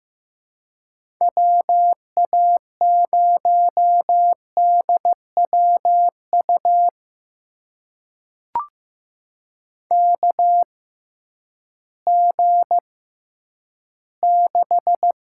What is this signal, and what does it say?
Morse code, heard as a keyed tone, about 15 wpm, spaced as for 10 wpm, 700 Hz. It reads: WA0DWU KG6
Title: USA Call Signs - 6 Characters - 6x Character Spacing - 15wpm